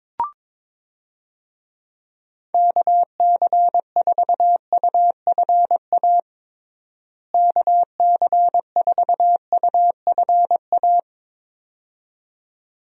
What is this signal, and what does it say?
Morse code, heard as a keyed tone, about 22 wpm, 700 Hz. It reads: KC4UFA KC4UFA
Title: USA Call Signs - 6 Characters - Repeated 2x - 22wpm